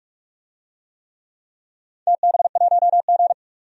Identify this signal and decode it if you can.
Morse code, heard as a keyed tone, about 45 wpm, 700 Hz. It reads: TB1G
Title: DXCC-100 Call Signs - 4 Characters - 45wpm